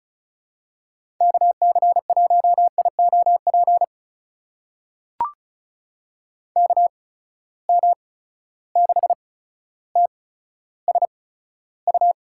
Morse code, heard as a keyed tone, about 36 wpm, spaced as for 15 wpm, 700 Hz. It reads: KC1IOP KM6TSU
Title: USA Call Signs - 6 Characters - 8x Character Spacing - 35wpm